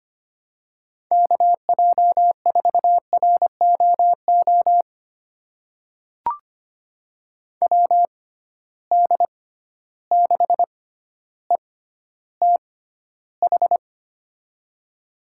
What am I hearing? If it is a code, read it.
Morse code, heard as a keyed tone, about 25 wpm, spaced as for 13 wpm, 700 Hz. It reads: KJ4ROO WD6ETH
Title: USA Call Signs - 6 Characters - 6x Character Spacing - 25wpm